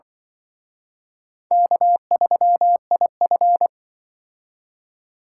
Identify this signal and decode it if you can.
Morse code, heard as a keyed tone, about 25 wpm, 700 Hz. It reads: K3IF
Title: Speed Racing - USA 1x2 Call Signs 20wpm